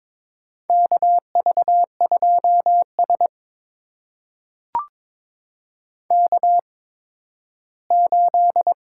Morse code, heard as a keyed tone, about 22 wpm, spaced as for 13 wpm, 700 Hz. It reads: KV2S K8
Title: USA Call Signs - 4 Characters - 8x Character Spacing - 22wpm